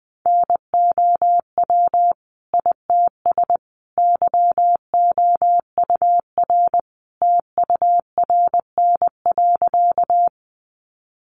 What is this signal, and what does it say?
Morse code, heard as a keyed tone, about 20 wpm, 700 Hz. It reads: NOW ITS YOUR TURN.
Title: Sentences from Top 600 Words 20wpm